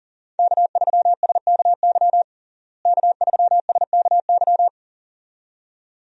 Morse code, heard as a keyed tone, about 40 wpm, 700 Hz. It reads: K3SKY K3SKY
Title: USA Call Signs - 5 Characters - Repeated 2x - 40wpm